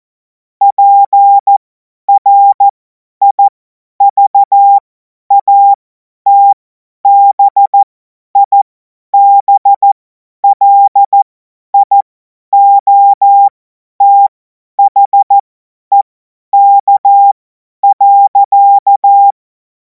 Morse code, heard as a keyed tone, about 14 wpm, spaced as for 10 wpm, 800 Hz. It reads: PRIVATBIBLIOTHEK.
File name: Morsekurs / Witze / Witz 05. Mai 2026 50 F70